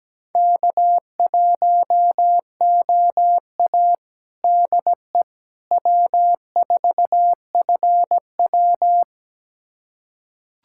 Morse code, heard as a keyed tone, about 17 wpm, 700 Hz. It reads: K1OA DE W4FW
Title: USA Callsign DE Callsign - 4 Characters - 17wpm